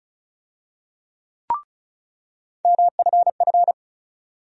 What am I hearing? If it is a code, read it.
Morse code, heard as a keyed tone, about 36 wpm, 700 Hz. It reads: MFF
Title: USA Call Sign Postfixes - 3 Letters 35wpm